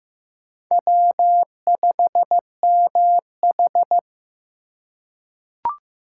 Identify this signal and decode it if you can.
Morse code, heard as a keyed tone, about 15 wpm, 700 Hz. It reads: W5MH
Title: USA Call Signs - 4 Characters - 4x Character Spacing - 15wpm